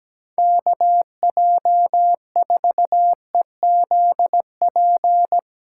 Morse code, heard as a keyed tone, about 17 wpm, 700 Hz. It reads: KJ4EZP
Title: USA Call Signs - 8x Character Spacing - 17wpm